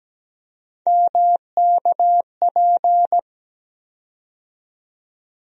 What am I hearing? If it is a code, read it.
Morse code, heard as a keyed tone, about 17 wpm, 700 Hz. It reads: MKP
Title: USA Call Sign Postfixes - 3 Letters 17wpm